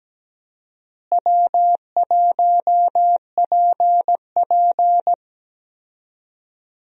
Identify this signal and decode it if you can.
Morse code, heard as a keyed tone, about 17 wpm, 700 Hz. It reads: W1PP
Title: USA Call Signs - 4 Characters - Repeated 2x - 17wpm